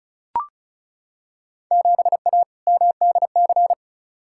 Morse code, heard as a keyed tone, about 36 wpm, 690 Hz. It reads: 7A MDC
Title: Field Day Exchanges Encore - 35wpm